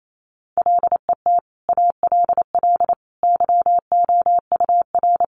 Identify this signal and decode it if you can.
Morse code, heard as a keyed tone, about 28 wpm, 700 Hz. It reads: LET ALL YOUR
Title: Sentences from Top 1500 Words 28wpm